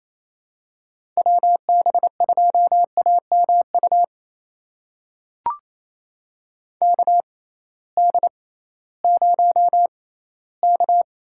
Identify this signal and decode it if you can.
Morse code, heard as a keyed tone, about 27 wpm, spaced as for 16 wpm, 700 Hz. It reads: WB2AMU KD0K
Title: USA Call Signs - 6 Characters - 6x Character Spacing - 28wpm